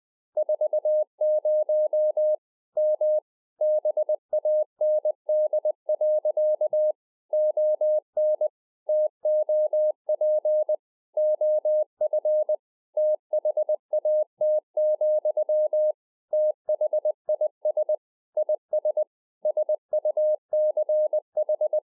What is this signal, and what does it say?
Morse code, heard as a keyed tone, about 20 wpm, 610 Hz. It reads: 40 M BAND. ON TOP OF THAT, THIS IS SUCH
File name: 2011-04-mp3 / A SMALL TRANCEIVER 3 20WPM